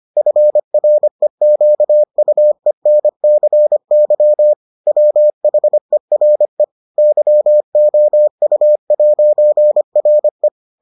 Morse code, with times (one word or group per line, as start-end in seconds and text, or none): 0.2-4.5 FREQUENCY
4.9-6.6 WHERE
7.0-10.5 YOU'RE